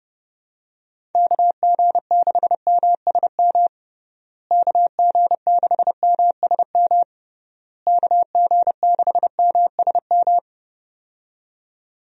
Morse code, heard as a keyed tone, about 31 wpm, 700 Hz. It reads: KG6MSM KG6MSM KG6MSM
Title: USA Call Signs - 6 Characters - Repeated 3x - 30wpm